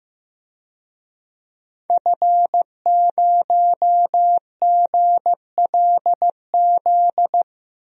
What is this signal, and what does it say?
Morse code, heard as a keyed tone, about 15 wpm, 700 Hz. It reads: F0GLZ
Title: DXCC-100 Call Signs - 5 Characters - 15wpm